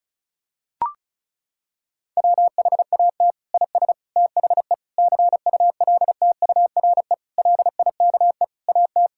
Morse code, heard as a keyed tone, about 36 wpm, 700 Hz. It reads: WHAT IS THE CULTURE LIKE AT
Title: Sentences from Top 1000 Words - Review Part 6 - 35wpm